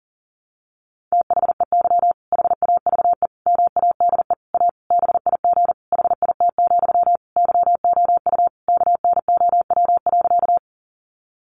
Morse code, heard as a keyed tone, about 40 wpm, 700 Hz. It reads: THEY HAVE MADE A BIG HIT, YOU KNOW.
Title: Sentences from Top 1200 Words 40wpm